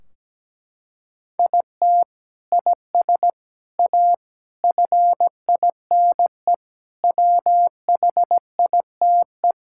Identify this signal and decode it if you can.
Morse code, heard as a keyed tone, about 17 wpm, 700 Hz. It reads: IT IS A FINE WHITE
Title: Sentences from Top 1600 Words 17wpm